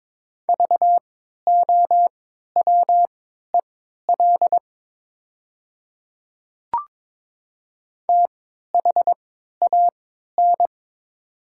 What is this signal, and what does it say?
Morse code, heard as a keyed tone, about 22 wpm, spaced as for 13 wpm, 700 Hz. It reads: VOWEL THAN
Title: 3x Character Spacing - Top 500 Words - 22wpm